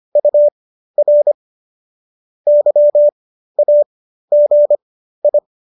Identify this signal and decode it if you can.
Morse code, heard as a keyed tone, about 25 wpm, spaced as for 13 wpm, 590 Hz. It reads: UR YAGI